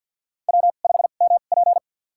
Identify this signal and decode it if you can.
Morse code, heard as a keyed tone, about 52 wpm, spaced as for 37 wpm, 700 Hz. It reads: W5MP
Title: USA Call Signs - 4 Characters - 2x Character Spacing - 50wpm